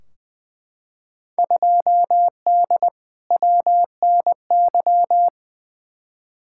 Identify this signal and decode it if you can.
Morse code, heard as a keyed tone, about 20 wpm, 700 Hz. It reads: 2D WNY
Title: Field Day Exchanges - 20wpm